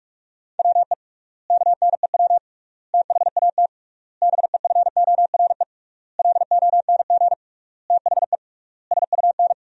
Morse code, heard as a keyed tone, about 41 wpm, 700 Hz. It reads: WE KNEW THAT BEFORE LONG THE SUN
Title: 3x Word-Spacing - Sentences from Top 400 Words - 45wpm